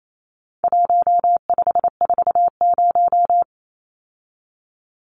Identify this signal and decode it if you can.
Morse code, heard as a keyed tone, about 28 wpm, 700 Hz. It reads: 1540
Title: Numbers - 4-Digit 28wpm